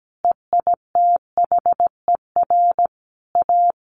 Morse code, heard as a keyed tone, about 17 wpm, 700 Hz. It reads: EITHER A
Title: Sentences from Top 600 Words 17wpm